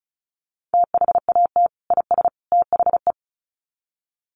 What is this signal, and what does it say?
Morse code, heard as a keyed tone, about 34 wpm, 700 Hz. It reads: THAT IS THE
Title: Sets of 3 Words 35wpm